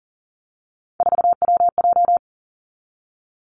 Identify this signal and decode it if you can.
Morse code, heard as a keyed tone, about 41 wpm, 700 Hz. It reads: 4WJ
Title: Three Challenging Letter-Number Combinations 40wpm